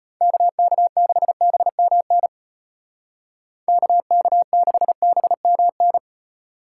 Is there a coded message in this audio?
Yes: Morse code from 0.2 to 6.0 s, heard as a keyed tone, about 34 wpm, 700 Hz. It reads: KK6BMN KK6BMN